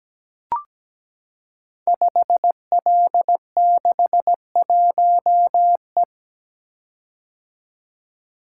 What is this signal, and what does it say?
Morse code, heard as a keyed tone, about 17 wpm, 700 Hz. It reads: 5L61E
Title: Code Groups - 5 Letter:Numbers - 17wpm